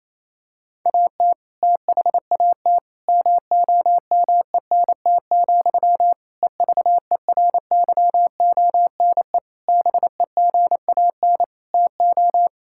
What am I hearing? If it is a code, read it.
Morse code, heard as a keyed tone, about 28 wpm, 700 Hz. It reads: AT THAT MOMENT, EVERYONE BEGAN TO
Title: Sentences from Top 1000 Words - Review Part 5 - 28wpm